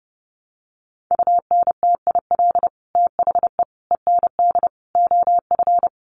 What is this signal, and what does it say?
Morse code, heard as a keyed tone, about 28 wpm, 700 Hz. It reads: UNTIL THE END OF
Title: Sets of 5 Words Encore 30wpm